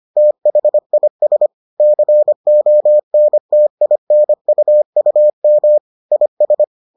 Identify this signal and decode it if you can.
Morse code, heard as a keyed tone, about 25 wpm, 600 Hz. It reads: THIS CONTINUUM IS